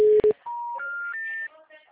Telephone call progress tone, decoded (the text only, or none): special information tone